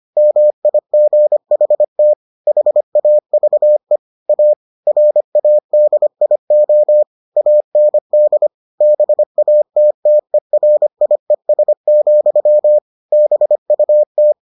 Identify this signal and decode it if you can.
Morse code, heard as a keyed tone, about 25 wpm, 600 Hz. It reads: MIGHT HAVE A RADIO AND BATTERIES, BUT